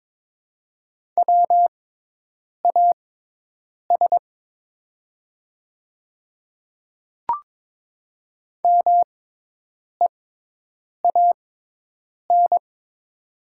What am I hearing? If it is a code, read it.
Morse code, heard as a keyed tone, about 22 wpm, spaced as for 8 wpm, 700 Hz. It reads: WAS MEAN
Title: 6x Character Spacing - Top 200 Words - 22wpm